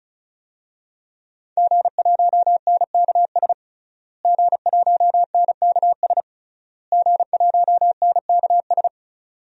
Morse code, heard as a keyed tone, about 34 wpm, 700 Hz. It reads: G1NKS G1NKS G1NKS
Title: DXCC-100 Call Signs - 5 Characters - Repeated 3x - 35wpm